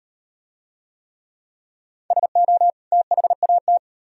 Morse code, heard as a keyed tone, about 35 wpm, 700 Hz. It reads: SO THAT